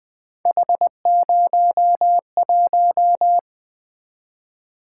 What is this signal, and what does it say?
Morse code, heard as a keyed tone, about 20 wpm, 700 Hz. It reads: H01
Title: Three Letter-Number Combinations - 20wpm